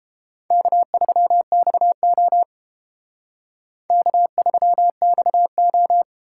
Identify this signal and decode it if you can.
Morse code, heard as a keyed tone, about 28 wpm, 700 Hz. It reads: K3XO K3XO